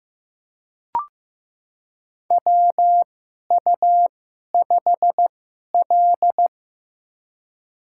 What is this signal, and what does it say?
Morse code, heard as a keyed tone, about 15 wpm, spaced as for 11 wpm, 700 Hz. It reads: WU5L
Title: USA Call Signs - 4 Characters - 2x Character Spacing - 15wpm